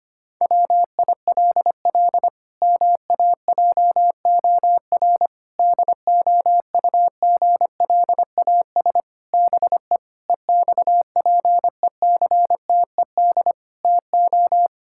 Morse code, heard as a keyed tone, about 25 wpm, 700 Hz. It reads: WILL MAJOR DOUGLAS BE EXPECTED TO
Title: Pangram Challenge v2 - 25wpm